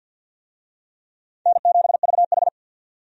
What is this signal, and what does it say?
Morse code, heard as a keyed tone, about 52 wpm, 700 Hz. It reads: N7VH
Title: USA Call Signs - 4 Characters - 50wpm